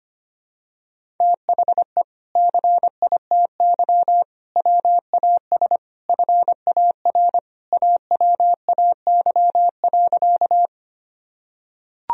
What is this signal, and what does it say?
Morse code, heard as a keyed tone, about 25 wpm, 700 Hz. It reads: THE CITY WAS FAR AWAY.
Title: Sentences from Top 1000 Words - Review Part 1 - 25wpm